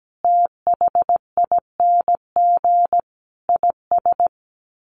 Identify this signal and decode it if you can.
Morse code, heard as a keyed tone, about 17 wpm, 700 Hz. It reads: THING IS
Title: Sets of 5 Words Encore 17wpm